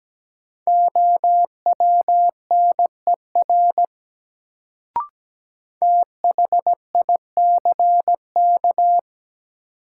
Morse code, heard as a keyed tone, about 17 wpm, 700 Hz. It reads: OWNER THICK
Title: Sentences from Top 1600 Words - New Word Drill Alt - 17wpm